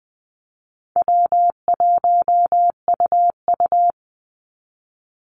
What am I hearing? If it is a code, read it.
Morse code, heard as a keyed tone, about 20 wpm, 700 Hz. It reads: W1UU
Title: Easy Call Signs Repeated 3x - 20wpm